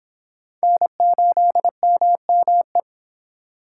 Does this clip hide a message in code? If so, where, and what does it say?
Yes: Morse code from 0.6 to 2.8 s, heard as a keyed tone, about 26 wpm, 700 Hz. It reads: N8MME